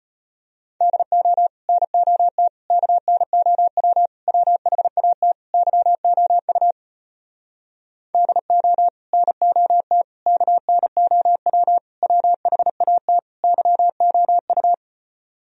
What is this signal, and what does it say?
Morse code, heard as a keyed tone, about 33 wpm, 700 Hz. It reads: DO NOT KNOW WHAT YOU DO NOT KNOW WHAT YOU